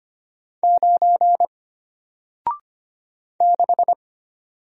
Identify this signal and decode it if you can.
Morse code, heard as a keyed tone, about 25 wpm, 700 Hz. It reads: 9 6
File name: Single-Letter-Number-25wpm